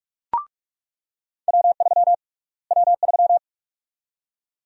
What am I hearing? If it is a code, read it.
Morse code, heard as a keyed tone, about 47 wpm, 700 Hz. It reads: W3 W3
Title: USA Call Sign Prefixes - Repeated 2x - 45wpm